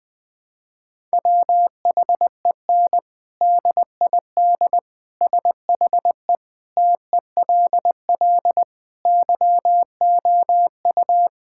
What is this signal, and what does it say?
Morse code, heard as a keyed tone, about 20 wpm, 700 Hz. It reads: WHEN DID SHE TELL YOU
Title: Sentences from Top 1000 Words - Review Part 6 - 20wpm